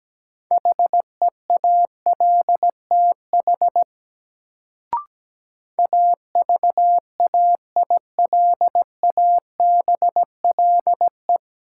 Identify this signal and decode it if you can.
Morse code, heard as a keyed tone, about 17 wpm, 700 Hz. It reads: HEALTH AVAILABLE
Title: Sentences from Top 600 Words - New Word Drill - 17wpm